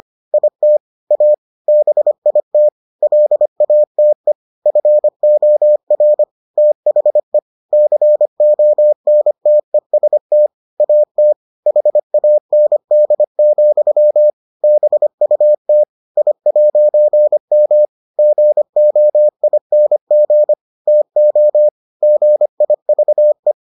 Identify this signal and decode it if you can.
Morse code, heard as a keyed tone, about 25 wpm, 600 Hz. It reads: IT A BIT LATE FOR THE CONTEST AT HAND, BUT I'M GOING TO GIVE